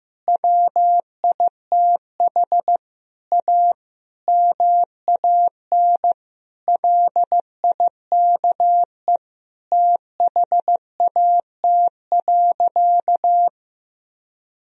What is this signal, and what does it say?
Morse code, heard as a keyed tone, about 15 wpm, 700 Hz. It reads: WITH A MAN LIKE THAT.
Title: Sentences from Top 1000 Words - Review Part 4 - 15wpm